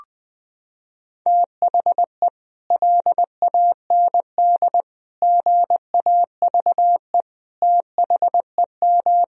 Morse code, heard as a keyed tone, about 20 wpm, 700 Hz. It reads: THE LAND GAVE THEM
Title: Sentences from Top 1000 Words - Review Part 1 - 20wpm